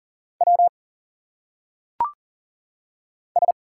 Morse code, heard as a keyed tone, about 40 wpm, 700 Hz. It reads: W S
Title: Single Letters - Phonetic Pronunciation 40wpm